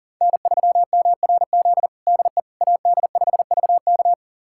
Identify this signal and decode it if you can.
Morse code, heard as a keyed tone, about 37 wpm, 700 Hz. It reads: N3MRZ DE AD5VK
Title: USA Callsign DE Callsign - 5 Characters - 40wpm